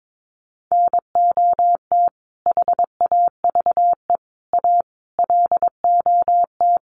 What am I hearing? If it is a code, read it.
Morse code, heard as a keyed tone, about 22 wpm, 700 Hz. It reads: NOT HAVE A LOT
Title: Sets of 5 Words 22wpm